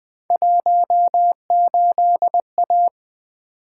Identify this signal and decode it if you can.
Morse code, heard as a keyed tone, about 20 wpm, 700 Hz. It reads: 18A
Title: Three Letter-Number Combinations - 20wpm